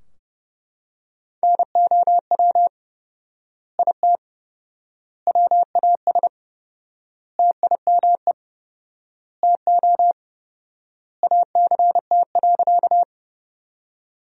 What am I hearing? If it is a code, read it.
Morse code, heard as a keyed tone, about 30 wpm, 700 Hz. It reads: NOW IT WAS TIME TO ACT.
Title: 4x Word-Spacing - Sentences from Top 200 Words - 30wpm